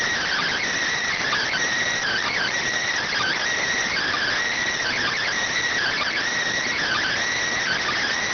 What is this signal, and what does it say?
Emergency Alert System (SAME) tones (header bursts), heard as harsh buzzing bursts. It is heard from 0.8 to 7.7 s.